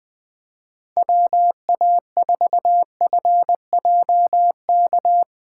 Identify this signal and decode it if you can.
Morse code, heard as a keyed tone, about 22 wpm, 700 Hz. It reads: WA4FJK
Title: Speed Racing - USA 2x3 Call Signs 20wpm